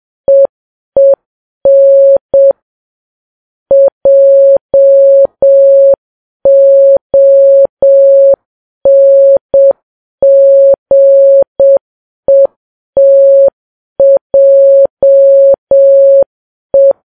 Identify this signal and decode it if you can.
Morse code, heard as a keyed tone, about 7 wpm, 550 Hz. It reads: EEN JONGETJE
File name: Files Web tls / Morse on9fv / morse7kps